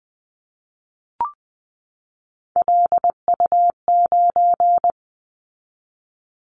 Morse code, heard as a keyed tone, about 19 wpm, 700 Hz. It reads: LU9
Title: Callsign Prefixes for Top 50 DXCC - 20wpm